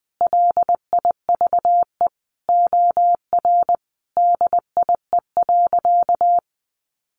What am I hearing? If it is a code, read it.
Morse code, heard as a keyed tone, about 20 wpm, 700 Hz. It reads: LIVE OR DIE.
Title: Sentences from Top 1300 Words 20wpm